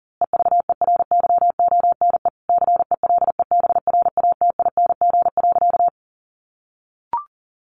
Morse code, heard as a keyed tone, about 40 wpm, 700 Hz. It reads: EVERYONE CELEBRATING.
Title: A Sound Of Thunder 40wpm